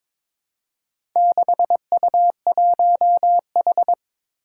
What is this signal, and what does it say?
Morse code, heard as a keyed tone, about 22 wpm, 700 Hz. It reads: 6U1H
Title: Four Challenging Letter-Number Combinations 22wpm